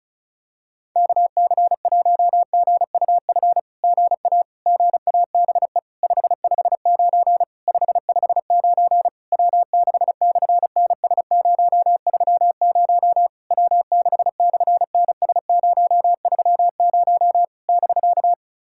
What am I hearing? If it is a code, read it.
Morse code, heard as a keyed tone, about 36 wpm, 700 Hz. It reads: KC1GUF GA GABE 559 559 W6/NS030 W6/NS030 <BK>
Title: SOTA QSO Practice 35wpm